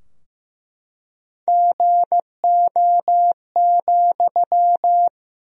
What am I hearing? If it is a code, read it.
Morse code, heard as a keyed tone, about 15 wpm, 700 Hz. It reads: GO,
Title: Sentences from Top 1000 Words - Review Part 3 - 15wpm